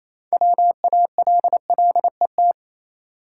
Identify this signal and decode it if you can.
Morse code, heard as a keyed tone, about 27 wpm, 690 Hz. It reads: WALLET
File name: Easy Words - Speed-Racing with Hints - 22wpm